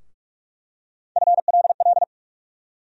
Morse code, heard as a keyed tone, about 46 wpm, 700 Hz. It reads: FLL
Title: USA Call Sign Postfixes - 3 Letters 45wpm